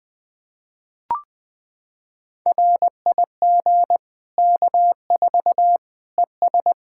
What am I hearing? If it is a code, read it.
Morse code, heard as a keyed tone, about 21 wpm, 700 Hz. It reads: RIG K4 ES
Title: Practice QSO - Exchange 2 - Info - 20wpm